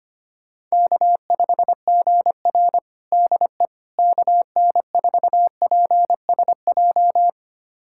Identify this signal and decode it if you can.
Morse code, heard as a keyed tone, about 25 wpm, 700 Hz. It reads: K5GR DE KN4PSJ
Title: USA Callsign DE Callsign 25wpm